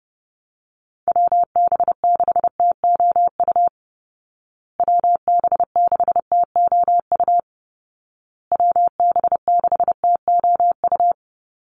Morse code, heard as a keyed tone, about 30 wpm, 700 Hz. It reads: WB6TOU WB6TOU WB6TOU
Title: Call Signs Repeated 2x - 30wpm